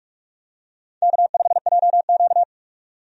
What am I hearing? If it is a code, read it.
Morse code, heard as a keyed tone, about 43 wpm, 700 Hz. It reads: K5JQ
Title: USA Call Signs - 4 Characters - 45wpm